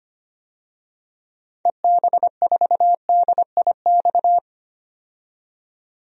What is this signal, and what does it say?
Morse code, heard as a keyed tone, about 25 wpm, 700 Hz. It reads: EB4DIX
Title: DXCC-100 Call Signs - 6 Characters - 25wpm